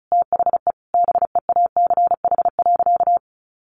Morse code, heard as a keyed tone, about 34 wpm, 700 Hz. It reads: THE BEACH.
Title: Sentences from Top 1500 Words 35wpm